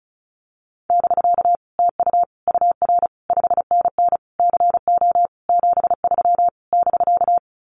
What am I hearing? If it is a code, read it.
Morse code, heard as a keyed tone, about 34 wpm, 700 Hz. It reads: <BK> TU UR 5NN CO 73 <BK>
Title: Common QSO Phrases Encore 35wpm